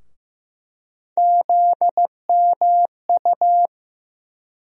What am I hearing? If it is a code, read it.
Morse code, heard as a keyed tone, about 15 wpm, 700 Hz. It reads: ZMU
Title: USA Call Sign Postfixes - 3 Letters 15wpm